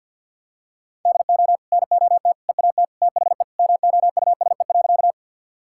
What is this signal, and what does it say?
Morse code, heard as a keyed tone, about 52 wpm, 700 Hz. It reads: DO NOT EAT THE MOUSE.
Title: Sentences from Top 1000 Words - Review Part 5 - 50wpm